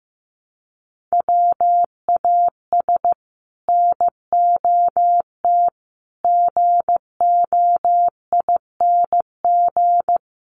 Morse code, heard as a keyed tone, about 15 wpm, 700 Hz. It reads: WAS NOT GOING
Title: Sets of 5 Words Encore 15wpm